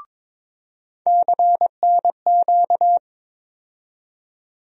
Morse code, heard as a keyed tone, about 22 wpm, 700 Hz. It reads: CNQ
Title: USA Call Sign Postfixes - 3 Letters 22wpm